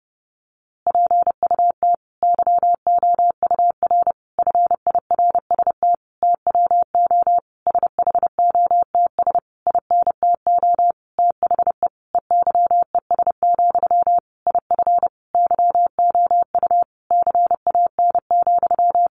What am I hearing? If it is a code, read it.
Morse code, heard as a keyed tone, about 30 wpm, 700 Hz. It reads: PUT YOUR FIRST TWO SHOTS INTO THE EYES, IF YOU CAN,
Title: A Sound Of Thunder 30wpm